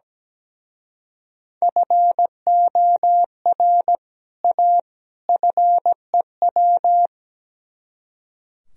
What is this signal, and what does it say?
Morse code, heard as a keyed tone, about 19 wpm, 700 Hz. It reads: FOR A FEW
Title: Speed-Racing - Sets of 3 Words 17wpm